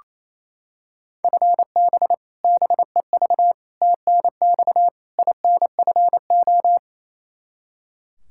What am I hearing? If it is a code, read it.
Morse code, heard as a keyed tone, about 28 wpm, 700 Hz. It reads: FB BEV TNX INFO
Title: Practice QSO - Exchange 3 - More Info - 28wpm